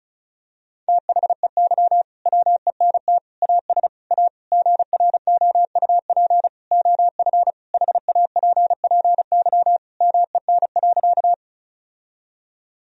Morse code, heard as a keyed tone, about 34 wpm, 700 Hz. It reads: THEY WENT AS A GROUP OF HAPPY MEN.
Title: Sentences from Top 1000 Words - Review Part 1 - 35wpm